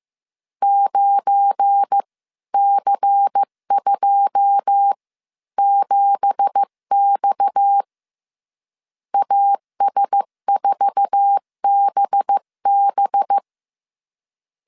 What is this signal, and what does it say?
Morse code, heard as a keyed tone, about 15 wpm, spaced as for 13 wpm, 800 Hz. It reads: 9C27X AS4BB
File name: CW / DET / 1 9 1500C S B 14wpm 3302